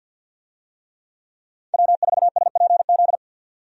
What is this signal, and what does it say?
Morse code, heard as a keyed tone, about 51 wpm, 700 Hz. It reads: W4SPZ
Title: USA Call Signs - 5 Characters - 50wpm